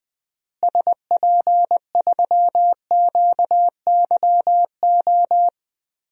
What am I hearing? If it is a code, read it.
Morse code, heard as a keyed tone, about 21 wpm, 700 Hz. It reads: SP3QYO
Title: DXCC-100 Call Signs - 6 Characters - Repeated 3x - 20wpm